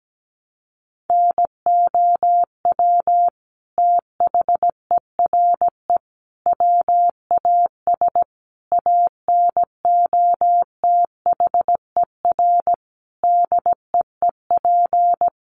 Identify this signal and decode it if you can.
Morse code, heard as a keyed tone, about 17 wpm, 700 Hz. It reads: NOW THERE WAS ANOTHER DEEP
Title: Sentences from Top 1300 Words 17wpm